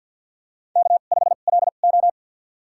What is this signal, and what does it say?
Morse code, heard as a keyed tone, about 47 wpm, spaced as for 35 wpm, 690 Hz. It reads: K5LO
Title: USA Call Signs - 2x Character Spacing - 50wpm